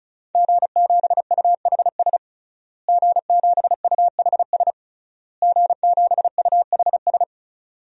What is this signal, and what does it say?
Morse code, heard as a keyed tone, about 34 wpm, 700 Hz. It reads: G7UHS G7UHS G7UHS
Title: DXCC-100 Call Signs - 5 Characters - Repeated 3x - 35wpm